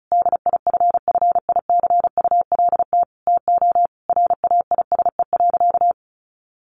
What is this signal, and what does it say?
Morse code, heard as a keyed tone, about 34 wpm, 700 Hz. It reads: DIFFICULT TO RAISE.
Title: Sentences from Top 1400 Words 35wpm